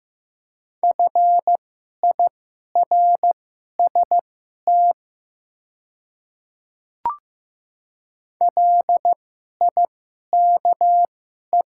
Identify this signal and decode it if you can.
Morse code, heard as a keyed tone, about 15 wpm, spaced as for 11 wpm, 700 Hz. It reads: FIRST LIKE
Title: 2x Character Spacing - Top 100 Words - 15wpm